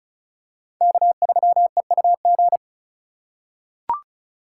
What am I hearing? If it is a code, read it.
Morse code, heard as a keyed tone, about 36 wpm, 700 Hz. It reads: K3EUG
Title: USA Call Signs - 2x Character Spacing - 35wpm